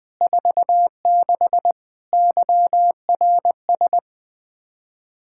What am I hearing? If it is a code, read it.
Morse code, heard as a keyed tone, about 20 wpm, 700 Hz. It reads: 46 YRS
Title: Practice QSO - Exchange 3 - More Info - 20wpm